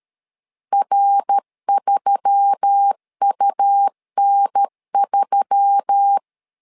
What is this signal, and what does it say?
Morse code, heard as a keyed tone, about 13 wpm, 800 Hz. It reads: R3UN3
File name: LEZIONI / 4 Q1S3 12wpm 2023